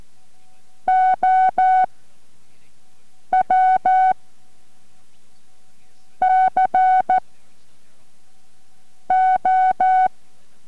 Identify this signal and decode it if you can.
Morse code, heard as a keyed tone, about 14 wpm, spaced as for 4 wpm, 750 Hz. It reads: OW CO
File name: MorseCode-5WPM 4